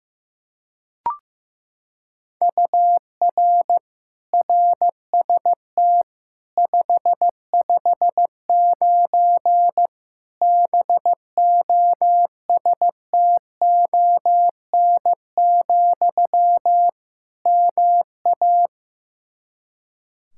Morse code, heard as a keyed tone, about 15 wpm, 700 Hz. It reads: UR RST 559 BOSTON, MA
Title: Practice QSO - Exchange 1 - Beginning - 15wpm